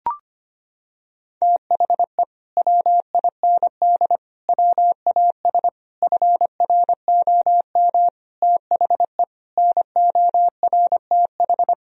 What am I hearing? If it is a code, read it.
Morse code, heard as a keyed tone, about 25 wpm, 700 Hz. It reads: THE WIND WAS FROM THE NORTH
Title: Sentences from Top 1000 Words - Review Part 2 - 25wpm